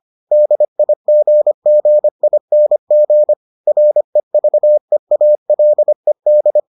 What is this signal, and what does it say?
Morse code, heard as a keyed tone, about 25 wpm, 600 Hz. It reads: DIGGING REVEALED